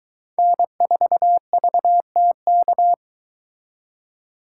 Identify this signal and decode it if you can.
Morse code, heard as a keyed tone, about 23 wpm, 700 Hz. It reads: N4VTK